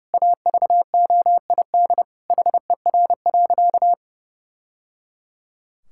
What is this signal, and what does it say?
Morse code, heard as a keyed tone, about 30 wpm, 700 Hz. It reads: AVOID HER.
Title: Sentences from Top 1000 Words - Review Part 6 - 30wpm